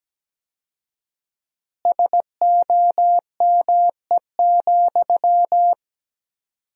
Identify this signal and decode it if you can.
Morse code, heard as a keyed tone, about 17 wpm, 700 Hz. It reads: SOME,
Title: 4x Word-Spacing - Sentences from Top 200 Words - 17wpm